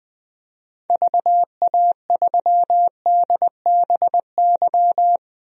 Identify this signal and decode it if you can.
Morse code, heard as a keyed tone, about 20 wpm, 700 Hz. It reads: VA3DBY
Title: DXCC-100 Call Signs - 6 Characters - 20wpm